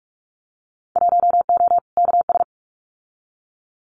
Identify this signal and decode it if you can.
Morse code, heard as a keyed tone, about 46 wpm, 700 Hz. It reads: 1O KS
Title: Winter Field Day Exchanges 45wpm